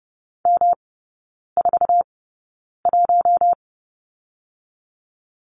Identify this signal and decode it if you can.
Morse code, heard as a keyed tone, about 30 wpm, spaced as for 9 wpm, 700 Hz. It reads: M41
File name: Lesson 18e - Letter m - Three Characters 30wpm